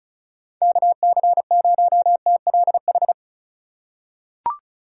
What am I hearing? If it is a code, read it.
Morse code, heard as a keyed tone, about 36 wpm, 700 Hz. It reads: KC0TLH
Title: USA Call Signs - 6 Characters - 3x Character Spacing - 35wpm